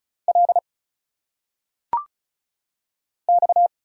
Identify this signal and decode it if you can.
Morse code, heard as a keyed tone, about 35 wpm, 700 Hz. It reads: L X
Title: Single Letters - Phonetic Pronunciation 35wpm